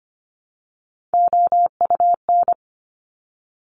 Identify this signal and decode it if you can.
Morse code, heard as a keyed tone, about 25 wpm, 700 Hz. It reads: OUN
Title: Top Three Letter Combinations in Words 25wpm